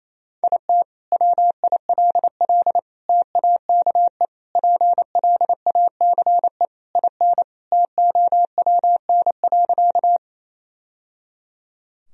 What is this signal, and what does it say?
Morse code, heard as a keyed tone, about 28 wpm, 700 Hz. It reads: IT WILL TAKE PLACE IN TOWN.
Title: Sentences from Top 1000 Words - Review Part 2 - 28wpm